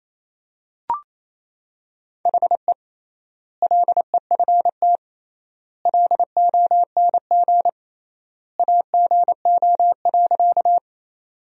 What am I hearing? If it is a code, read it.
Morse code, heard as a keyed tone, about 28 wpm, 700 Hz. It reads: HE LEFT LONG AGO.
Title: 3x Word-Spacing - Sentences from Top 500 Words - 28wpm